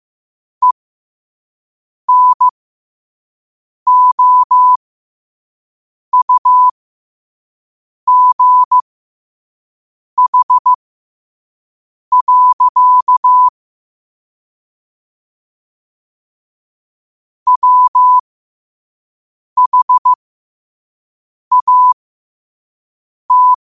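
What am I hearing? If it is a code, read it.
Morse code, heard as a keyed tone, about 15 wpm, spaced as for 5 wpm, 1000 Hz. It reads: ENOUGH. WHAT
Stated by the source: Morse code sent at 5 wpm.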